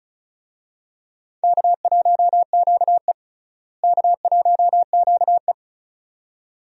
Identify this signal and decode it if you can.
Morse code, heard as a keyed tone, about 36 wpm, 700 Hz. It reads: K1QE K1QE
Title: USA Call Signs - 4 Characters - Repeated 2x - 35wpm